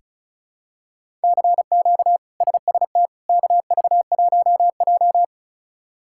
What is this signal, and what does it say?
Morse code, heard as a keyed tone, about 36 wpm, 700 Hz. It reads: CQ SST KV1J
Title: K1USN SST QSO Practice - 35wpm